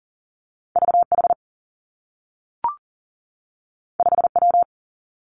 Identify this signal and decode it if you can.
Morse code, heard as a keyed tone, about 41 wpm, 700 Hz. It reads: V H 5 W
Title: Two Challenging Letter-Number Combinations 40wpm